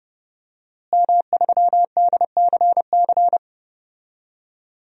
Morse code, heard as a keyed tone, about 30 wpm, 700 Hz. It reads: M3DCC
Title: DXCC-100 Call Signs - 5 Characters - Repeated 3x - 30wpm